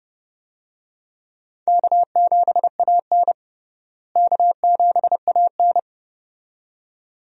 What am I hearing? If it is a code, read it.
Morse code, heard as a keyed tone, about 29 wpm, 700 Hz. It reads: K7AN K7AN
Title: USA Call Signs - 4 Characters - Repeated 2x - 30wpm